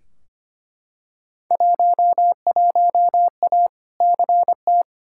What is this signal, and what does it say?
Morse code, heard as a keyed tone, about 25 wpm, 700 Hz. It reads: 11A CT
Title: Field Day Exchanges Encore - 25wpm